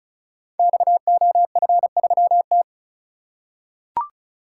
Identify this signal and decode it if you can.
Morse code, heard as a keyed tone, about 36 wpm, 690 Hz. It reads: XOF3T
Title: Code Groups - 5 Letter:Numbers - 35wpm